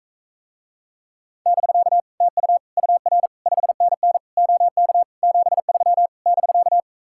Morse code, heard as a keyed tone, about 45 wpm, 700 Hz. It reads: <BK> TU UR 5NN OK 73 <BK>